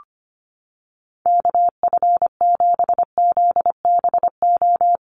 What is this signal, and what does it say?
Morse code, heard as a keyed tone, about 25 wpm, 700 Hz. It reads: KF7ZBO
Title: USA Call Signs Repeated 3x 25wpm